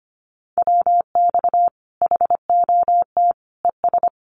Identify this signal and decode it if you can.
Morse code, heard as a keyed tone, about 25 wpm, 700 Hz. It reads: WX HOT ES
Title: Common QSO Phrases 25wpm